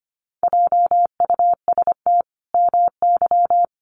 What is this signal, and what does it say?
Morse code, heard as a keyed tone, about 25 wpm, 700 Hz. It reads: JUST MY
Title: Sentences from Top 1200 Words 25wpm